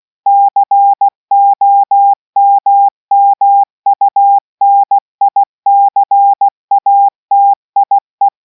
Morse code, heard as a keyed tone, about 16 wpm, 800 Hz. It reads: COMMUNICATIE